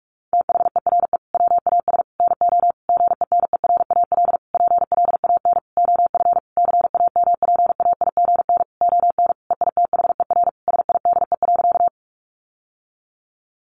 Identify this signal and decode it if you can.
Morse code, heard as a keyed tone, about 44 wpm, 700 Hz. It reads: THERE WAS NO GENERAL PLAN OF CAMPAIGN ON EITHER SIDE.
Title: Sentences from Top 1200 Words 45wpm